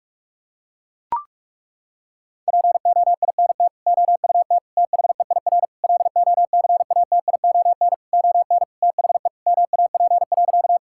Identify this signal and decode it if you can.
Morse code, heard as a keyed tone, about 42 wpm, 700 Hz. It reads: POINT OUT THEIR LOCATION ON THE MAP.
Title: Sentences from Top 1000 Words - Review Part 6 - 45wpm